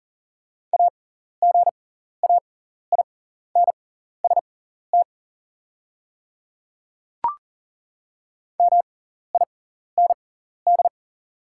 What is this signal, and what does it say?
Morse code, heard as a keyed tone, about 40 wpm, spaced as for 14 wpm, 690 Hz. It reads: AGAINST MIND
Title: 6x Character Spacing - Top 500 Words - 40wpm